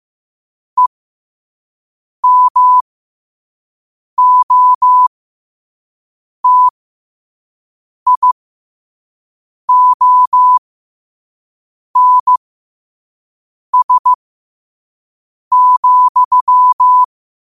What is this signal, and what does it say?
Morse code, heard as a keyed tone, about 15 wpm, spaced as for 5 wpm, 1000 Hz. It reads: EMOTIONS,
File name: qotdfiles / 5wpm